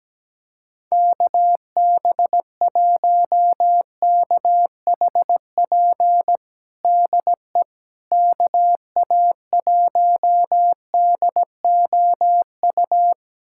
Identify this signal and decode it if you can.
Morse code, heard as a keyed tone, about 17 wpm, 700 Hz. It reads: KB1KHP DE KA1DOU
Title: USA Callsign DE Callsign 17wpm